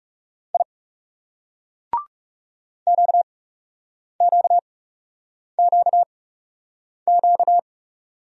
Morse code, heard as a keyed tone, about 38 wpm, spaced as for 8 wpm, 700 Hz. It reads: I QQQQ